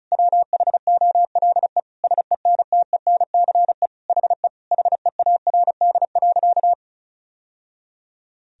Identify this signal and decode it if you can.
Morse code, heard as a keyed tone, about 34 wpm, 700 Hz. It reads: WHOLE SENTENCE HE HEARD.
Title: Sentences from Top 1000 Words - Review Part 1 - 35wpm